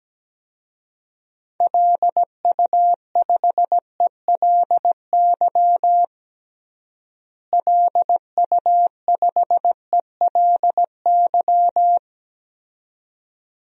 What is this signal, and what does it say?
Morse code, heard as a keyed tone, about 17 wpm, 700 Hz. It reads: LU5ELY LU5ELY
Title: DXCC-100 Call Signs - 6 Characters - Repeated 2x - 17wpm